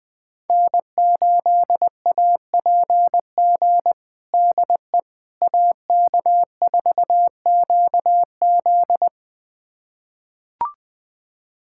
Morse code, heard as a keyed tone, about 20 wpm, 700 Hz. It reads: N8APG DE AK4QZ
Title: USA Callsign DE Callsign - 5 Characters - 20wpm